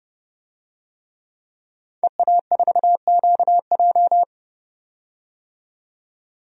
Morse code, heard as a keyed tone, about 32 wpm, 700 Hz. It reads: EA4QJ
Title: DXCC-100 Call Signs - 5 Characters - 30wpm